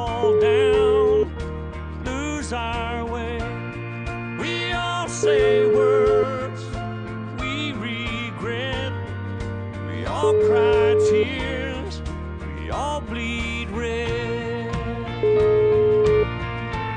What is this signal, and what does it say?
Ringback tone (Europe, 425 Hz), with noise.